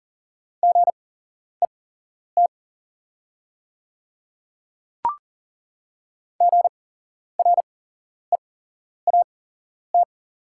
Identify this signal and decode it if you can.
Morse code, heard as a keyed tone, about 40 wpm, spaced as for 11 wpm, 690 Hz. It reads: GET GREAT
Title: 8x Character Spacing - Top 200 Words - 40wpm